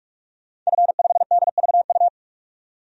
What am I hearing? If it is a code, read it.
Morse code, heard as a keyed tone, about 46 wpm, 700 Hz. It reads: F5DVU
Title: DXCC-100 Call Signs - 5 Characters - 45wpm